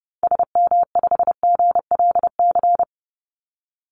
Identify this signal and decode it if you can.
Morse code, heard as a keyed tone, about 30 wpm, 700 Hz. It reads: SM5GLC
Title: Call Signs - 30wpm